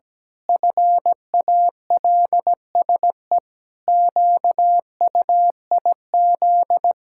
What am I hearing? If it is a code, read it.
Morse code, heard as a keyed tone, about 17 wpm, 700 Hz. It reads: FALSE QUIZ
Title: Pangram Challenge v2 - 17wpm